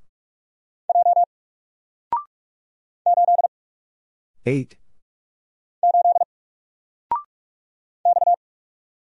Morse code, heard as a keyed tone, about 44 wpm, 700 Hz. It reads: J 8 8 X
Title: Single Letter-Number Alt 45wpm